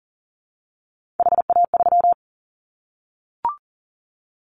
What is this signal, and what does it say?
Morse code, heard as a keyed tone, about 40 wpm, 700 Hz. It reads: HA3
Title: Callsign Prefixes for Top 50 DXCC - 40wpm